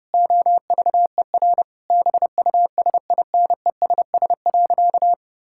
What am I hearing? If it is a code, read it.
Morse code, heard as a keyed tone, about 32 wpm, 700 Hz. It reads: OVER BUSINESS.
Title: Sentences from Top 1000 Words - Review Part 5 - 30wpm